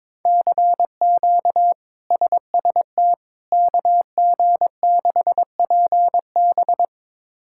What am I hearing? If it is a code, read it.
Morse code, heard as a keyed tone, about 21 wpm, 700 Hz. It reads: CQ SST KG6PB
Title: K1USN SST QSO Practice - 22wpm